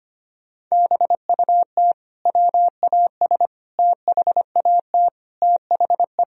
Morse code, heard as a keyed tone, about 25 wpm, 700 Hz. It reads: BUT WAS THAT THE
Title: Sentences from Top 1000 Words - Review Part 2 - 25wpm